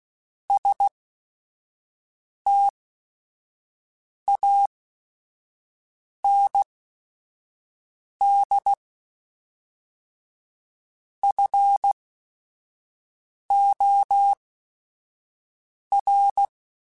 Morse code, heard as a keyed tone, about 16 wpm, spaced as for 5 wpm, 800 Hz. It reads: STAND FOR
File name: Quotes / Quotes-mp3s / StandFor @5